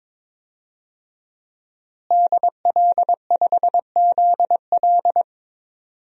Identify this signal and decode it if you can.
Morse code, heard as a keyed tone, about 22 wpm, 700 Hz. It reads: DL5ZL
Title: DXCC-100 Call Signs - 5 Characters - Repeated 3x - 22wpm